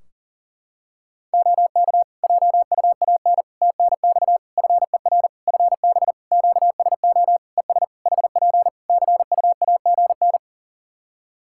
Morse code, heard as a keyed tone, about 40 wpm, 700 Hz. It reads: OK JUAN TNX FER FB QSO ES HP CUAGN
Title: K1USN SST QSO Practice 40wpm